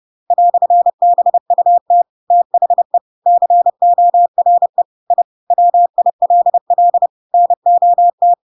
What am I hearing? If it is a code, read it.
Morse code, heard as a keyed tone, about 30 wpm, 700 Hz. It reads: "BUT THE CORE I WILL NOT
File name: Pinocchio / Pinocchio ch7 30 30 ik1hge